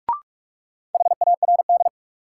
Morse code, heard as a keyed tone, about 46 wpm, 700 Hz. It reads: HARD
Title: Sentences from Top 300 Words - New Word Drill - 45wpm